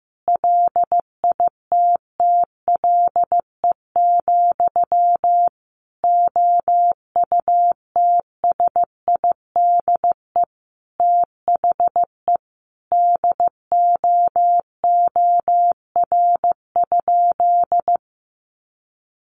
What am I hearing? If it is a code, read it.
Morse code, heard as a keyed tone, about 15 wpm, 700 Hz. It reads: LITTLE, OUTSIDE THE DOOR?
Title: Sentences from Top 900 Words 15wpm